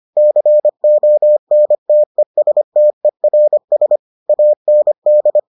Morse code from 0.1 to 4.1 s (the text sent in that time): CONTESTERS